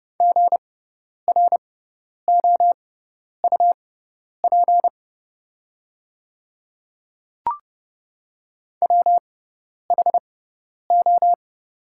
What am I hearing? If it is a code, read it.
Morse code, heard as a keyed tone, about 30 wpm, spaced as for 10 wpm, 690 Hz. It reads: GROUP WHO
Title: 6x Character Spacing - Top 500 Words - 30wpm